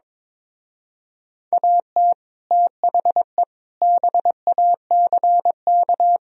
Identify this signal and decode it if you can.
Morse code, heard as a keyed tone, about 23 wpm, 700 Hz. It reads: AT THE BACK
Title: Speed-Racing - Sets of 5 Words 22wpm